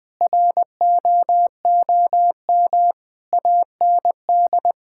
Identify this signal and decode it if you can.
Morse code, heard as a keyed tone, about 20 wpm, 700 Hz. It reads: ROOM AND
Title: Sentences from Top 1000 Words - Review Part 6 - 20wpm